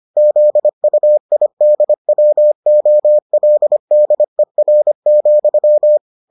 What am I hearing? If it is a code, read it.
Morse code, heard as a keyed tone, about 25 wpm, 600 Hz. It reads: ZUIDWOLDER,